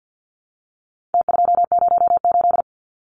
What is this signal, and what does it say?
Morse code, heard as a keyed tone, about 55 wpm, 700 Hz. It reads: T208
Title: Numbers - Contest Style - 4-Digit 50wpm